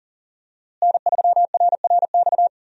Morse code, heard as a keyed tone, about 40 wpm, 700 Hz. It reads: N3RRX
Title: USA Call Signs - 5 Characters - Repeated 2x - 40wpm